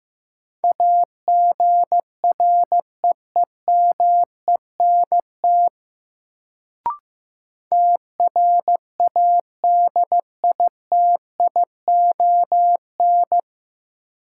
Morse code, heard as a keyed tone, about 15 wpm, 700 Hz. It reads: AGREEMENT TRADITION
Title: Sentences from Top 1500 Words - New Word Drill - 15wpm